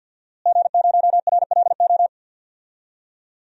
Morse code, heard as a keyed tone, about 47 wpm, 700 Hz. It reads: G0RLO
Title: DXCC-100 Call Signs - 5 Characters - Repeated 3x - 50wpm